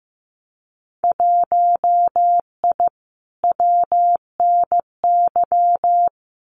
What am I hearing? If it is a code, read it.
Morse code, heard as a keyed tone, about 15 wpm, 700 Hz. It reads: 1I WNY
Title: Winter Field Day Exchanges Encore 15wpm